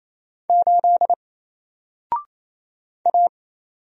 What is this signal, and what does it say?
Morse code, heard as a keyed tone, about 27 wpm, 700 Hz. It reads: 8 A
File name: Single-Letter-Number-28wpm